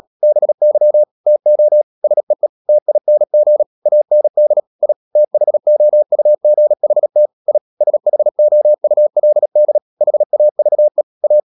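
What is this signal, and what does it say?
Morse code, heard as a keyed tone, about 37 wpm, 600 Hz. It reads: BY TO SEE TING AND I THOUGHT I SHOULD HAVE A
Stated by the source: Morse code practice at 35–39 wpm sending English text of Ken Follett's novel Never, from CW med Gnister.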